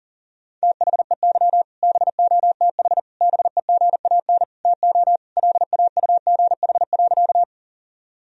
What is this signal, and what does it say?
Morse code, heard as a keyed tone, about 40 wpm, 700 Hz. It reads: THEY BOTH BEGAN TO LAUGH.
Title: Sentences from Top 1000 Words - Review Part 1 - 40wpm